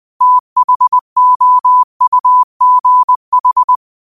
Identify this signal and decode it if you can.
Morse code, heard as a keyed tone, about 20 wpm, 1000 Hz. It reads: THOUGH
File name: qotdfiles / 20wpm